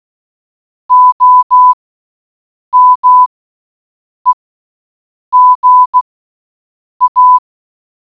Morse code, heard as a keyed tone, about 16 wpm, spaced as for 7 wpm, 1000 Hz. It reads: OMEGA